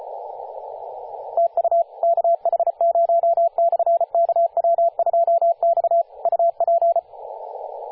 Morse code, heard as a keyed tone, about 35 wpm, 680 Hz, with radio noise. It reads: TU KH0/KW2X UP
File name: posts / 2016-sep-22-0537z-kh0-kw2x-18-077mhz-cw